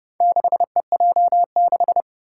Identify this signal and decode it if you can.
Morse code, heard as a keyed tone, about 31 wpm, 700 Hz. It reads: 6EJ6
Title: Four Challenging Letter-Number Combinations 30wpm